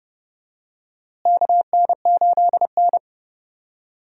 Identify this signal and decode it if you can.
Morse code, heard as a keyed tone, about 30 wpm, 700 Hz. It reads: KN8N
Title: USA Call Signs - 4 Characters - Repeated 2x - 30wpm